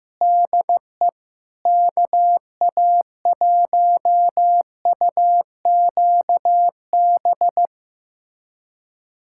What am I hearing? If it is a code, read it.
Morse code, heard as a keyed tone, about 15 wpm, 700 Hz. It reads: DE KA1UQB
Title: USA Callsign DE Callsign - 6 Characters - 15wpm